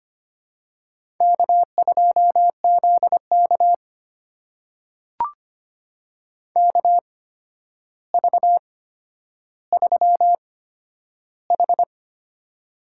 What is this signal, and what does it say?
Morse code, heard as a keyed tone, about 25 wpm, spaced as for 11 wpm, 700 Hz. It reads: K2ZK KV3H
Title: USA Call Signs - 4 Characters - 8x Character Spacing - 25wpm